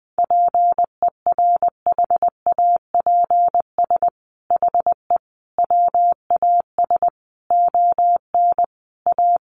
Morse code, heard as a keyed tone, about 20 wpm, 700 Hz. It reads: PERHAPS HE WAS ON A
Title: Sentences from Top 1100 Words 20wpm